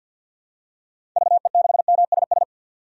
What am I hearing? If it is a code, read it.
Morse code, heard as a keyed tone, about 52 wpm, 700 Hz. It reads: VE6MSS
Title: DXCC-100 Call Signs - 6 Characters - 50wpm